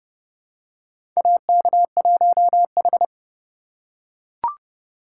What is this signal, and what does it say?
Morse code, heard as a keyed tone, about 29 wpm, 700 Hz. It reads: AK1H
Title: USA Call Signs - 4 Characters - 2x Character Spacing - 30wpm